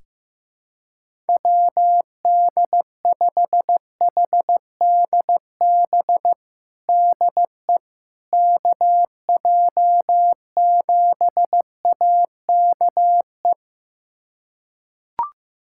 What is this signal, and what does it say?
Morse code, heard as a keyed tone, about 15 wpm, 700 Hz. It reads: WD5HDB DE KJ7AKE
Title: USA Callsign DE Callsign - 6 Characters - 15wpm